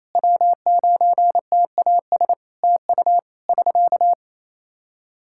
Morse code, heard as a keyed tone, about 27 wpm, 700 Hz. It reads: W9TAS TU <SK>
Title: Practice QSO - Exchange 4 - Ending - 28wpm